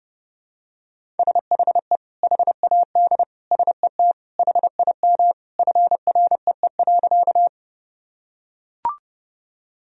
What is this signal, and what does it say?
Morse code, heard as a keyed tone, about 30 wpm, 700 Hz. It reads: SHE HAD SET HIM FREE.
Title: Sentences from Top 1000 Words - Review Part 1 - 30wpm